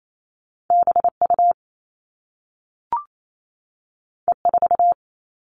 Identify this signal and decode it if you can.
Morse code, heard as a keyed tone, about 28 wpm, 700 Hz. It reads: BU E4
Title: All Cut Numbers - 2-Digit 28wpm